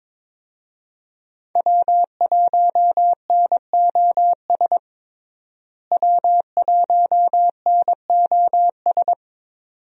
Morse code, heard as a keyed tone, about 22 wpm, 700 Hz. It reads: W1NOS W1NOS
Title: USA Call Signs - 5 Characters - Repeated 2x - 22wpm